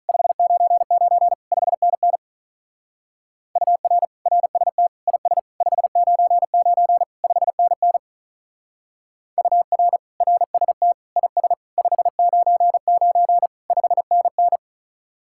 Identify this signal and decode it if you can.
Morse code, heard as a keyed tone, about 43 wpm, 700 Hz. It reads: 599 5NN UR RST IS 599 5NN UR RST IS 599 5NN
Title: Speed-Racing Common QSO Phrases Encore - 35wpm